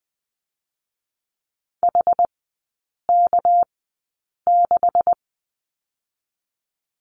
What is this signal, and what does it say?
Morse code, heard as a keyed tone, about 20 wpm, 700 Hz. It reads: H K 6
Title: 218 Lesson 35e - Number 0 - Three Characters 20wpm